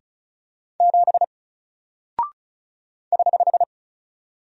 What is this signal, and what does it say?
Morse code, heard as a keyed tone, about 34 wpm, 700 Hz. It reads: 7 <HH>
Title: Single Character (Letter:Number:Punctuation:Prosign) Alt 35wpm